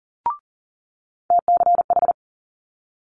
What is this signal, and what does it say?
Morse code, heard as a keyed tone, about 38 wpm, 700 Hz. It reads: TCH
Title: Multi-letter Phonemes Quick Fire 40wpm